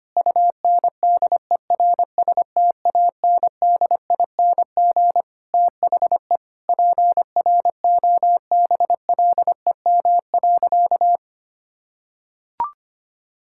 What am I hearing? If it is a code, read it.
Morse code, heard as a keyed tone, about 25 wpm, 700 Hz. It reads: UNDERSTANDING THE PROBLEM.
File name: practice / 25wpm / Sentences with 9-Word Phrases - 25wpm